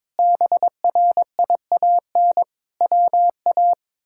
Morse code, heard as a keyed tone, about 22 wpm, 700 Hz. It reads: BRIAN WA
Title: K1USN SST QSO Practice - 22wpm